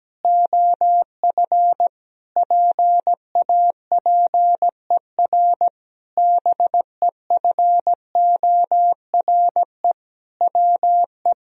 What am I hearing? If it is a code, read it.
Morse code, heard as a keyed tone, about 17 wpm, 700 Hz. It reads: OF PAPER BEFORE WE
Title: Sentences from Top 1000 Words - Review Part 2 - 17wpm